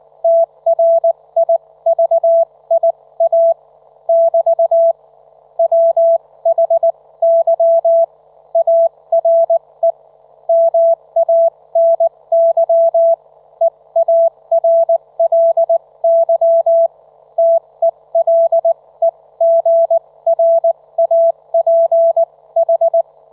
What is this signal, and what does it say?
Morse code, heard as a keyed tone, about 19 wpm, spaced as for 17 wpm, 670 Hz. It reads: TRIVIA <BT> WHY ARE MANY EARLY TELEGRAPH
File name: Trivia